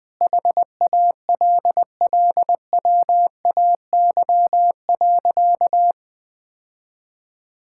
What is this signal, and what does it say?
Morse code, heard as a keyed tone, about 20 wpm, 700 Hz. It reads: HALLWAY.
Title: Sentences with 4-Word Phrases - 20wpm